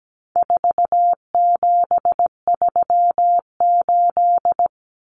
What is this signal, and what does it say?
Morse code, heard as a keyed tone, about 17 wpm, 700 Hz. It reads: 4738
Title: Numbers - 4-Digit 17wpm